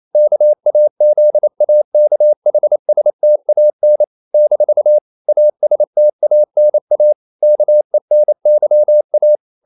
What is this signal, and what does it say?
Morse code, heard as a keyed tone, about 28 wpm, 600 Hz. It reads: KAZAKHSTAN - ASTANA KENYA
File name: CW / Practice / NationalCapitals28